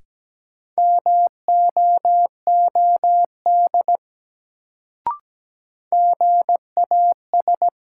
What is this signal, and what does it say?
Morse code, heard as a keyed tone, about 17 wpm, 700 Hz. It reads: MOOD GAS
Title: Sentences from Top 1100 Words - New Word Drill - 17wpm